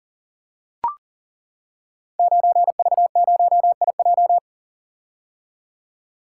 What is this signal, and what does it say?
Morse code, heard as a keyed tone, about 43 wpm, 690 Hz. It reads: 9V0IJ
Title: Code Groups - 5 Letter:Numbers - 40wpm